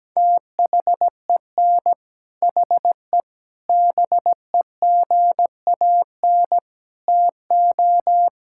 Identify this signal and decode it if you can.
Morse code, heard as a keyed tone, about 17 wpm, 700 Hz. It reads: THEN HE BEGAN TO
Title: Sentences from Top 1000 Words - Review Part 5 - 17wpm